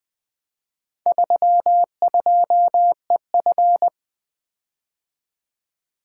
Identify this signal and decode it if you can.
Morse code, heard as a keyed tone, about 20 wpm, 700 Hz. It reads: 32EF
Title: Four Letter-Number Combinations - 20wpm